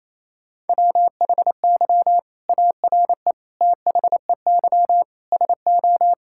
Morse code, heard as a keyed tone, about 28 wpm, 700 Hz. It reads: WHY ARE THEY SO
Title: Sentences from Top 1000 Words - Review Part 6 - 28wpm